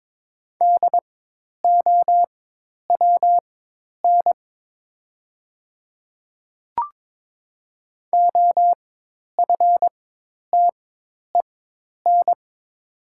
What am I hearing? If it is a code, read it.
Morse code, heard as a keyed tone, about 22 wpm, spaced as for 10 wpm, 700 Hz. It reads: DOWN OFTEN
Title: 4x Character Spacing - Top 500 Words - 22wpm